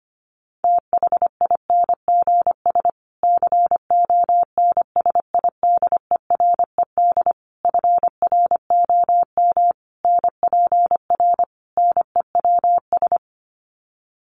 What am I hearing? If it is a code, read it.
Morse code, heard as a keyed tone, about 25 wpm, 700 Hz. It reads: THINGS CONSIDERED FROM NPR NEWS
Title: Sets of 5 Words Encore 25wpm